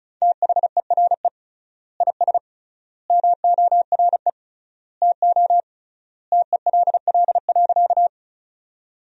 Morse code, heard as a keyed tone, about 36 wpm, 700 Hz. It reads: THERE IS MORE TO TELL.
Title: 3x Word-Spacing - Sentences from Top 200 Words - 35wpm